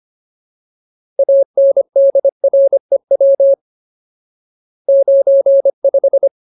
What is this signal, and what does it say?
Morse code, heard as a keyed tone, about 25 wpm, 550 Hz. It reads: ANDREW 95
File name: Ultimate Word List / 25WPM Ultimate